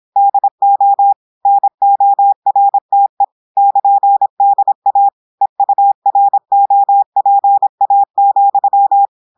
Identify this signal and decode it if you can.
Morse code, heard as a keyed tone, about 26 wpm, 800 Hz. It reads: DO NORTE <KN>DA EUROPA,
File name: audios / portugais / portugais texte en clair 1 vitesse 26 mots minutes